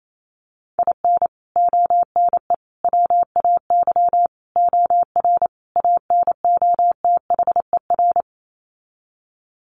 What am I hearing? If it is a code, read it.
Morse code, heard as a keyed tone, about 28 wpm, 700 Hz. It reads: IN ONE WAY OR ANOTHER
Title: Sets of 5 Words Encore 28wpm